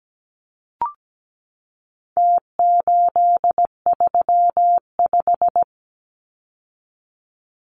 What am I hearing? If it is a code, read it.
Morse code, heard as a keyed tone, about 17 wpm, 700 Hz. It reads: T835
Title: Numbers - Contest Style - 4-Digit 17wpm